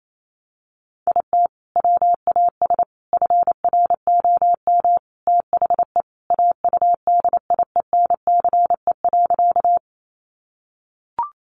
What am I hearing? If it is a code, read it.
Morse code, heard as a keyed tone, about 28 wpm, 700 Hz. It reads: IT WAS FROM THE AUDIENCE.
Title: Sentences from Top 800 Words 28wpm